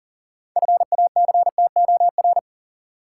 Morse code, heard as a keyed tone, about 40 wpm, 700 Hz. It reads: FACTOR
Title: Six Letter Words - 40wpm